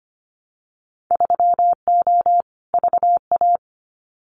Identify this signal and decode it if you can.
Morse code, heard as a keyed tone, about 25 wpm, 700 Hz. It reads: 3O VA
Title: Winter Field Day Exchanges 25wpm